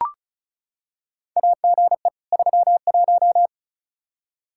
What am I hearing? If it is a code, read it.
Morse code, heard as a keyed tone, about 36 wpm, 700 Hz. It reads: AGE 31
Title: Practice QSO - Exchange 3 - More Info - 35wpm